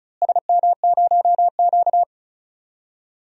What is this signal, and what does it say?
Morse code, heard as a keyed tone, about 36 wpm, 700 Hz. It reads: SM0Q
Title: DXCC-100 Call Signs - 4 Characters - Repeated 2x - 35wpm